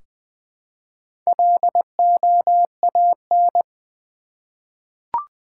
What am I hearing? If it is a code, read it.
Morse code, heard as a keyed tone, about 20 wpm, 700 Hz. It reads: LOAN
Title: Four Letter Words - 20wpm